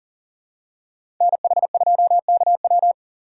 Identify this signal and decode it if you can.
Morse code, heard as a keyed tone, about 38 wpm, 690 Hz. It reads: NH2KW
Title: USA Call Signs - 4x Character Spacing - 40wpm